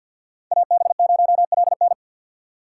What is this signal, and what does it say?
Morse code, heard as a keyed tone, about 46 wpm, 700 Hz. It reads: AB0LN
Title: USA Call Signs - 5 Characters - 4x Character Spacing - 50wpm